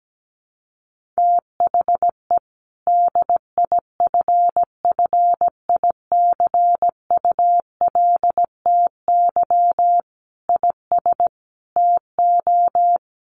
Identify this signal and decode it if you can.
Morse code, heard as a keyed tone, about 17 wpm, 700 Hz. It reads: THE DIFFICULTY IS TO
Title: Sentences from Top 1500 Words 17wpm